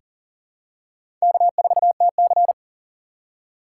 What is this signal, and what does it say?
Morse code, heard as a keyed tone, about 37 wpm, 700 Hz. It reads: K4TC
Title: USA Call Signs - 3x Character Spacing - 40wpm